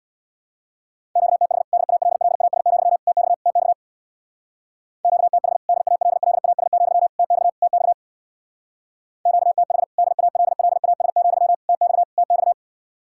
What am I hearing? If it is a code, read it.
Morse code, heard as a keyed tone, about 64 wpm, 700 Hz. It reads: QTH DALLAS, TX TX QTH DALLAS, TX TX QTH DALLAS, TX TX